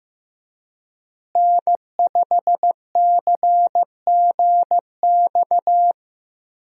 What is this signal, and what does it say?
Morse code, heard as a keyed tone, about 15 wpm, 700 Hz. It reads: N5CGX
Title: USA Call Signs - 5 Characters - Repeated 3x - 15wpm